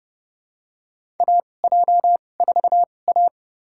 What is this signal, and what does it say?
Morse code, heard as a keyed tone, about 30 wpm, spaced as for 22 wpm, 700 Hz. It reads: AJ4A
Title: USA Call Signs - 4 Characters - 2x Character Spacing - 30wpm